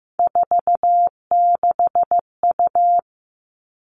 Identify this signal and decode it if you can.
Morse code, heard as a keyed tone, about 15 wpm, 700 Hz. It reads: 46U
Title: All Cut Numbers - 3-Digit 15wpm